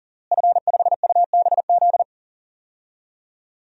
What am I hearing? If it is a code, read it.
Morse code, heard as a keyed tone, about 40 wpm, 700 Hz. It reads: F5UBZ
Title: DXCC-100 Call Signs - 5 Characters - Repeated 3x - 40wpm